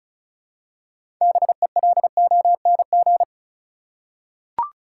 Morse code, heard as a keyed tone, about 34 wpm, 700 Hz. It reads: BELONG
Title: Six Letter Words - 35wpm